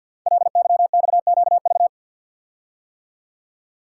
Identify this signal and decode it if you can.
Morse code, heard as a keyed tone, about 52 wpm, 700 Hz. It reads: LYXQV
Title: Code Groups - 5 Letters - 50wpm